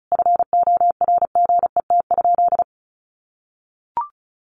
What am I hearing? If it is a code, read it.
Morse code, heard as a keyed tone, about 34 wpm, 700 Hz. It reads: FORGET?
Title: Sentences from Top 1200 Words 35wpm